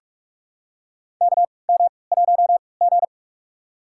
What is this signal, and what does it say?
Morse code, heard as a keyed tone, about 44 wpm, spaced as for 25 wpm, 690 Hz. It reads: KM1G
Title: USA Call Signs - 4 Characters - 3x Character Spacing - 45wpm